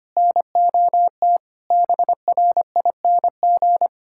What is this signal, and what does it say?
Morse code, heard as a keyed tone, about 25 wpm, 700 Hz. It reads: NOT BRING
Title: Sentences from Top 1600 Words 25wpm